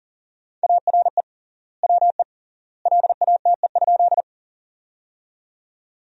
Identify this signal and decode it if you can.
Morse code, heard as a keyed tone, about 40 wpm, 700 Hz. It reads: ARE WE LATE?
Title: 3x Word-Spacing - Sentences from Top 300 Words - 40wpm